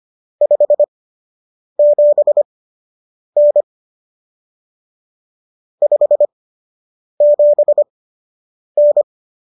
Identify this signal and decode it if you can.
Morse code, heard as a keyed tone, about 25 wpm, spaced as for 8 wpm, 590 Hz. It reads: 57N 57N